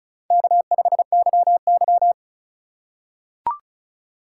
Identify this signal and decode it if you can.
Morse code, heard as a keyed tone, about 34 wpm, 700 Hz. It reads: K5YY
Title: USA Call Signs - 4 Characters - 3x Character Spacing - 35wpm